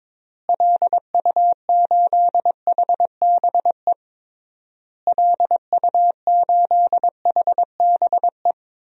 Morse code, heard as a keyed tone, about 22 wpm, 700 Hz. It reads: LU8HBE LU8HBE
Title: DXCC-100 Call Signs - 6 Characters - Repeated 3x - 22wpm